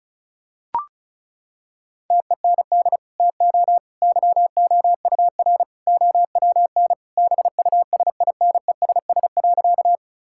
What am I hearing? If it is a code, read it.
Morse code, heard as a keyed tone, about 34 wpm, 700 Hz. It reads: TEND TO YOUR OWN BUSINESS.
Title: Sentences from Top 1000 Words - Review Part 5 - 35wpm